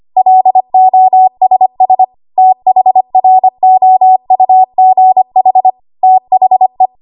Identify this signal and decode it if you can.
Morse code, heard as a keyed tone, about 25 wpm, 750 Hz. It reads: LOSS THROUGH THE
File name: Archive / 25 WPM / 25WPM